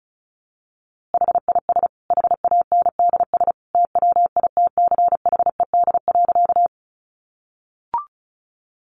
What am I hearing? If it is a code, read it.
Morse code, heard as a keyed tone, about 34 wpm, 700 Hz. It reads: HIS HANDS TWITCHED.
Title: A Sound Of Thunder 35wpm